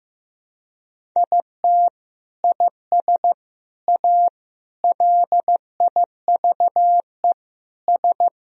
Morse code, heard as a keyed tone, about 15 wpm, 700 Hz. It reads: IT IS A LIVE S
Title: Sentences from Top 1000 Words - Review Part 1 - 15wpm